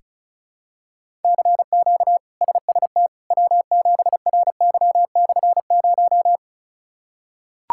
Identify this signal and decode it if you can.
Morse code, heard as a keyed tone, about 36 wpm, 700 Hz. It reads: CQ SST W7RY/0
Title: K1USN SST QSO Practice - 35wpm